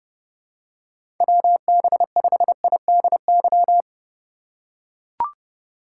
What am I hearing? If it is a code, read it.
Morse code, heard as a keyed tone, about 28 wpm, 700 Hz. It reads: WB5IDY
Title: K1USN SST QSO Practice - 30wpm